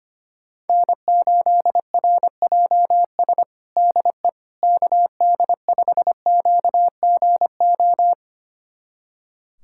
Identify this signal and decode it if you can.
Morse code, heard as a keyed tone, about 25 wpm, 700 Hz. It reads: N8RJS DE KD5QGO
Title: USA Callsign DE Callsign 25wpm